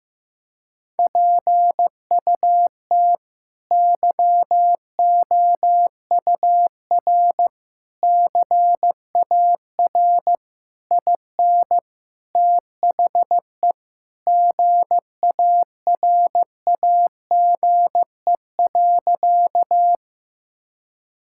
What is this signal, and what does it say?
Morse code, heard as a keyed tone, about 15 wpm, 700 Hz. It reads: PUT YOUR CAR IN THE GARAGE.
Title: Sentences from Top 1600 Words 15wpm